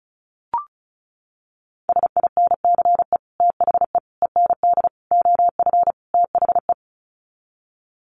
Morse code, heard as a keyed tone, about 34 wpm, 700 Hz. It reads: SINCE THE END OF THE
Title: Sets of 5 Words Encore 35wpm